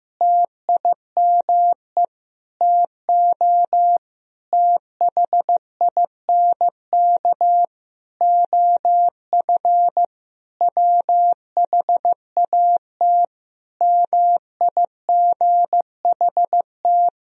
Morse code, heard as a keyed tone, about 16 wpm, 700 Hz. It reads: TIME TO THINK OF WHAT MIGHT
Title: Sentences from Top 400 Words and Abbreviations 15wpm